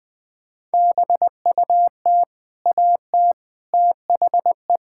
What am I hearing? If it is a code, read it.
Morse code, heard as a keyed tone, about 21 wpm, 700 Hz. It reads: BUT AT THE
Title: Sentences with 8-Word Phrases - Spoken Beginning - 20wpm